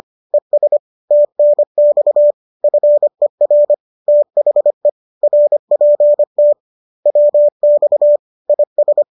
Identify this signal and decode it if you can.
Morse code, heard as a keyed tone, about 25 wpm, 590 Hz. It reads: ES TNX FER THE RPT WX IS